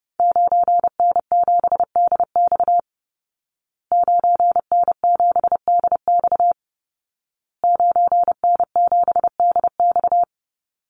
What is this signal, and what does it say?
Morse code, heard as a keyed tone, about 30 wpm, 700 Hz. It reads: 9N7DX 9N7DX 9N7DX
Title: Call Signs Repeated 2x - 30wpm